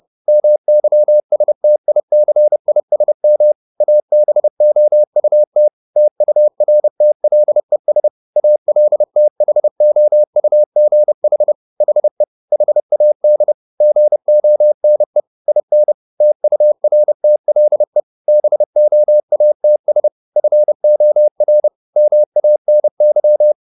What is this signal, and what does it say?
Morse code, heard as a keyed tone, about 30 wpm, 600 Hz. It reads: MYSTICISM ABOUT TURTLES ALTHOUGH HE HAD GONE IN TURTLE BOATS FOR MANY
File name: the old man and the sea / the old man and the sea 1 30wpm